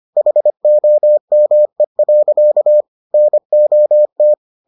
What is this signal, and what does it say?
Morse code, heard as a keyed tone, about 25 wpm, 600 Hz. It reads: HOME. NOT